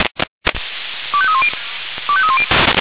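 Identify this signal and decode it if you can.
Five-tone selective call (ZVEI), heard as quick runs of five tones, tones of about 70 ms. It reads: 25310 25310
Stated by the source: ZVEI five-tone call 25305 received by SDR.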